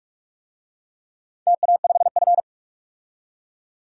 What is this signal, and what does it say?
Morse code, heard as a keyed tone, about 41 wpm, 700 Hz. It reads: TA5F
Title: DXCC-100 Call Signs - 4 Characters - 45wpm